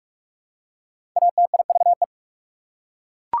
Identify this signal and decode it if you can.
Morse code, heard as a keyed tone, about 48 wpm, 690 Hz. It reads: ATIVE
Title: Prefixes and Suffixes 45wpm